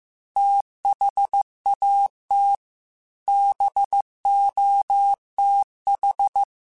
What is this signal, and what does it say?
Morse code, heard as a keyed tone, about 15 wpm, 800 Hz. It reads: THAT BOTH
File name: Technical Subjects / Technical Subjects-mp3s / Reactance @13